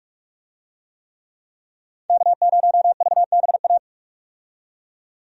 Morse code, heard as a keyed tone, about 46 wpm, 700 Hz. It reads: K0VBA
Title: USA Call Signs - 5 Characters - 45wpm